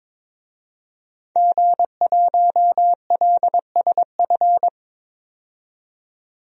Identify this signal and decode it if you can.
Morse code, heard as a keyed tone, about 22 wpm, 700 Hz. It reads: G1LSF
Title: DXCC-100 Call Signs - 5 Characters - 22wpm